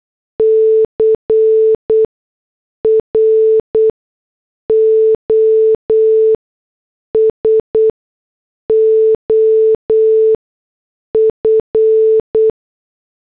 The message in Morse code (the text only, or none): CROSOF